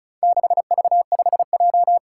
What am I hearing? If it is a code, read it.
Morse code, heard as a keyed tone, about 36 wpm, 700 Hz. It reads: 6V5J
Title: Four Challenging Letter-Number Combinations 35wpm